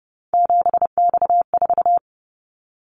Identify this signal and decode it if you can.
Morse code, heard as a keyed tone, about 30 wpm, 700 Hz. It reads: 7X4
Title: Callsign Prefixes for Top 100 DXCC - 30wpm